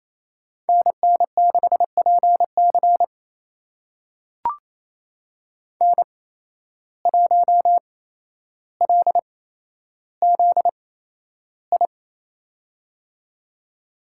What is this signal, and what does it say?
Morse code, heard as a keyed tone, about 28 wpm, spaced as for 12 wpm, 700 Hz. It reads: NN6PC N1LZI
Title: USA Call Signs - 8x Character Spacing - 28wpm